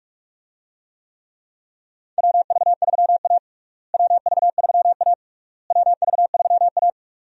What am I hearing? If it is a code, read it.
Morse code, heard as a keyed tone, about 46 wpm, 700 Hz. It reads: WV3A WV3A WV3A
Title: USA Call Signs - 4 Characters - Repeated 3x - 45wpm